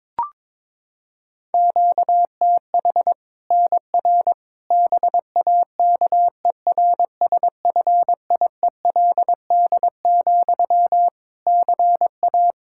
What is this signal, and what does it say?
Morse code, heard as a keyed tone, about 22 wpm, 700 Hz. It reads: QTH NR BAKERSFIELD, CA
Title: K1USN SST QSO Practice 22wpm